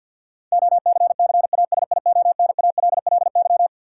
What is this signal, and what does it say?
Morse code, heard as a keyed tone, about 53 wpm, 700 Hz. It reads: OCCASIONALLY
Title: Sentences from Top 1300 Words - New Word Drill Alt - 50wpm